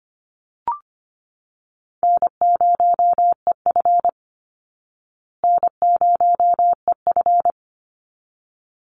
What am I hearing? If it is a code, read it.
Morse code, heard as a keyed tone, about 25 wpm, 700 Hz. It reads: N0EF N0EF
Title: Easy Call Signs Repeated 2x - 25wpm